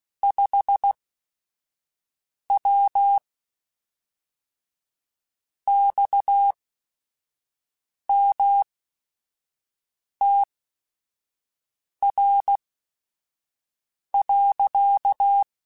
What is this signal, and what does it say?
Morse code, heard as a keyed tone, about 16 wpm, spaced as for 5 wpm, 800 Hz. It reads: 5W XMTR.
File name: QSOs / QSOs-mp3s / WA7QOX 1@5